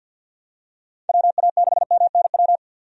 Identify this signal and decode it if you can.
Morse code, heard as a keyed tone, about 50 wpm, 700 Hz. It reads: WA6MNW
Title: USA Call Signs - 6 Characters - 50wpm